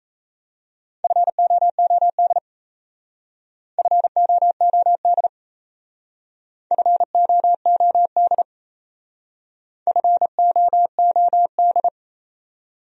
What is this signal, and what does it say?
Morse code, heard as a keyed tone, about 34 wpm, 700 Hz. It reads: FOOD FOOD FOOD FOOD
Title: Speed-Racing Top 500 Words 28wpm